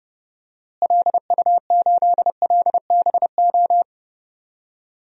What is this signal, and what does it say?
Morse code, heard as a keyed tone, about 30 wpm, 700 Hz. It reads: LU8LBO
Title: DXCC-100 Call Signs - 6 Characters - Repeated 2x - 30wpm